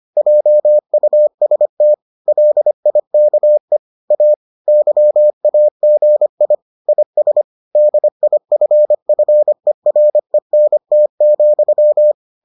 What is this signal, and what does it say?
Morse code, heard as a keyed tone, about 25 wpm, 600 Hz. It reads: JUST LIKE A YAGI IS DIFFERENT,